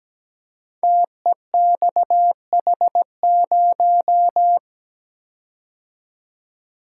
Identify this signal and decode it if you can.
Morse code, heard as a keyed tone, about 17 wpm, 700 Hz. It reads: TEXH0
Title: Code Groups - 5 Letter:Numbers - 17wpm